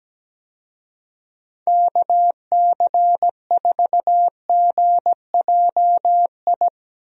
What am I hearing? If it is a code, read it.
Morse code, heard as a keyed tone, about 17 wpm, 700 Hz. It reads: KC4GJI
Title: USA Call Signs - 6 Characters - Repeated 2x - 17wpm